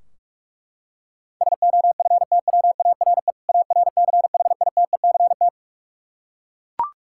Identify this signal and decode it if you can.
Morse code, heard as a keyed tone, about 41 wpm, 700 Hz. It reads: SOFTWARE ARCHITECT
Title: Practice QSO - Exchange 3 - More Info - 45wpm